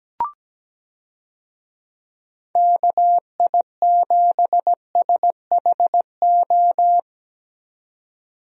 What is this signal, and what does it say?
Morse code, heard as a keyed tone, about 17 wpm, 700 Hz. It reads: KI7SHO
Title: USA Call Signs - 6 Characters - 17wpm